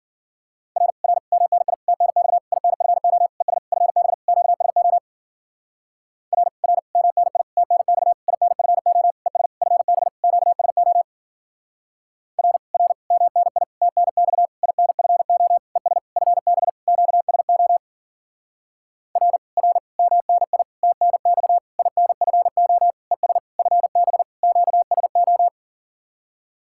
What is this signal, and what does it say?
Morse code, heard as a keyed tone, about 52 wpm, 700 Hz. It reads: R R MNI TNX INFO ES FB QSO R R MNI TNX INFO ES FB QSO R R MNI TNX INFO ES FB QSO R R MNI TNX INFO ES FB QSO